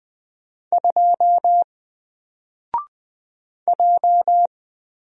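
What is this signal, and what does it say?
Morse code, heard as a keyed tone, about 20 wpm, 700 Hz. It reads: 2 J
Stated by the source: Single letter and number Morse drill at 20 wpm, from N4LCD.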